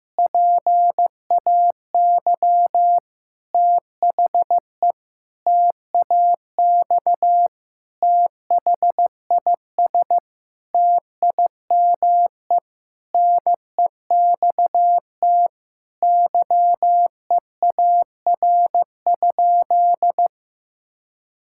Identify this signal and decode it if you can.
Morse code, heard as a keyed tone, about 15 wpm, 700 Hz. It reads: PAY THE TAX THIS TIME NEXT YEAR?
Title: Sentences from Top 1000 Words - Review Part 5 - 15wpm